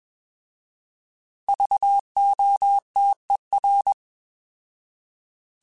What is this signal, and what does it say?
Morse code, heard as a keyed tone, about 21 wpm, 800 Hz. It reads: VOTER
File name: Alphabetical / Alphabetical mp3s / Vs @20